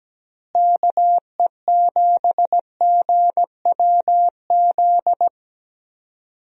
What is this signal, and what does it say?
Morse code, heard as a keyed tone, about 17 wpm, 700 Hz. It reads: KE7GWZ
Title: USA Call Signs - 6 Characters - Repeated 3x - 17wpm